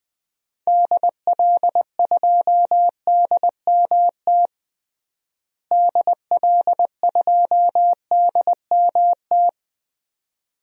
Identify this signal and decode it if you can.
Morse code, heard as a keyed tone, about 20 wpm, 700 Hz. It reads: DL2DMT DL2DMT
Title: DXCC-100 Call Signs - 6 Characters - Repeated 2x - 20wpm